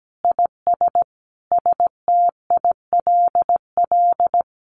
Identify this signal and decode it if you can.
Morse code, heard as a keyed tone, about 17 wpm, 700 Hz. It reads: IS STILL
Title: Sentences from Top 1500 Words 17wpm